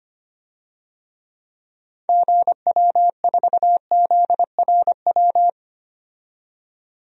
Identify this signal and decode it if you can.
Morse code, heard as a keyed tone, about 25 wpm, 700 Hz. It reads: GW4ZRW
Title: DXCC-100 Call Signs - 6 Characters - 25wpm